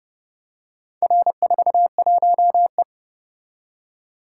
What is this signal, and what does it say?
Morse code, heard as a keyed tone, about 28 wpm, 700 Hz. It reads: R41E